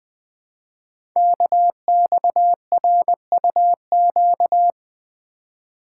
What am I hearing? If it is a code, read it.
Morse code, heard as a keyed tone, about 21 wpm, 700 Hz. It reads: KXRUQ
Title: Code Groups - 5 Letters - 20wpm